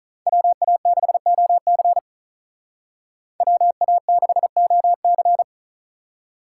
Morse code, heard as a keyed tone, about 43 wpm, 700 Hz. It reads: WA6OC WA6OC
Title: Speed Racing - USA 2x2 Call Signs 35wpm